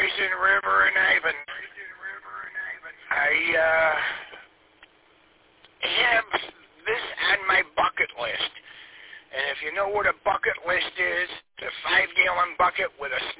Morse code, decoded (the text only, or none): none